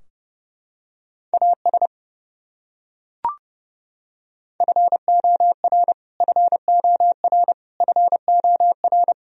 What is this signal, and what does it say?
Morse code, heard as a keyed tone, about 30 wpm, 700 Hz. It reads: AS FOR FOR FOR
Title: Top 100 Words Repeated 30wpm